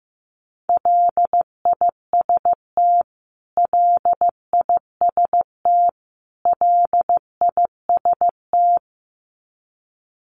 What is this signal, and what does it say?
Morse code, heard as a keyed tone, about 15 wpm, 700 Hz. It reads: LIST LIST LIST
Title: Top 500 Words Repeated 15wpm